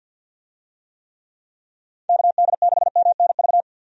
Morse code, heard as a keyed tone, about 47 wpm, 700 Hz. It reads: KD6MNV
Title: USA Call Signs - 6 Characters - 50wpm